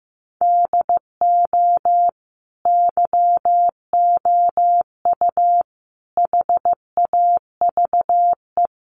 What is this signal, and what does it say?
Morse code, heard as a keyed tone, about 15 wpm, 700 Hz. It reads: DO YOU HAVE
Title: Sentences from Top 1200 Words 15wpm